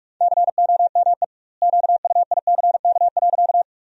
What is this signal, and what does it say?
Morse code, heard as a keyed tone, about 46 wpm, 700 Hz. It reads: COME QUICK.
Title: 2x Word-Spacing - Sentences from Top 500 Words - 45wpm